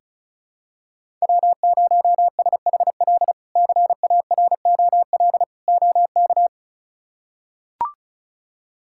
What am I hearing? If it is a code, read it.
Morse code, heard as a keyed tone, about 36 wpm, 700 Hz. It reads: W0SHL CAROL OK
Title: K1USN SST QSO Practice - 35wpm